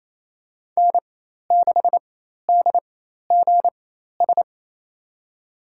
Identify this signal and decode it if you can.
Morse code, heard as a keyed tone, about 27 wpm, spaced as for 13 wpm, 700 Hz. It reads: N6DGS
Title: USA Call Signs - 4x Character Spacing - 28wpm